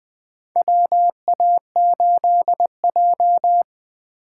Morse code, heard as a keyed tone, about 20 wpm, 700 Hz. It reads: WA8J
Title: USA Call Signs - 4 Characters - 6x Character Spacing - 20wpm